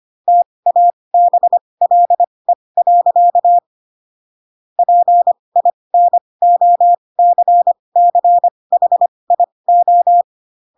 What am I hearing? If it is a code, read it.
Morse code, heard as a keyed tone, about 25 wpm, spaced as for 20 wpm, 700 Hz. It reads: TABLE. PINOCCHIO
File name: Pinocchio / Pinocchio ch7 15 25 ik1hge